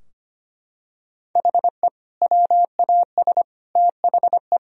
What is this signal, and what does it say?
Morse code, heard as a keyed tone, about 25 wpm, 700 Hz. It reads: HE WAS THE
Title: Sentences from Top 1000 Words - Review Part 3 - 25wpm